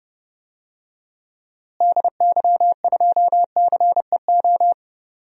Morse code, heard as a keyed tone, about 30 wpm, 700 Hz. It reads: DY2CEO
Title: DXCC-100 Call Signs - 6 Characters - Repeated 3x - 30wpm